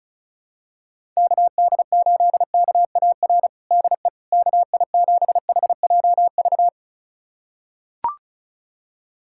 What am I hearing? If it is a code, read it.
Morse code, heard as a keyed tone, about 34 wpm, 700 Hz. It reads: KD8KAR DE KI7HJV
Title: USA Callsign DE Callsign - 6 Characters - 35wpm